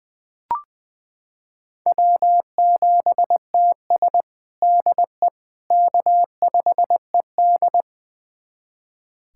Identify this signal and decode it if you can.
Morse code, heard as a keyed tone, about 20 wpm, 700 Hz. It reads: W7TS DE K5ED
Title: USA Callsign DE Callsign - 4 Characters - 20wpm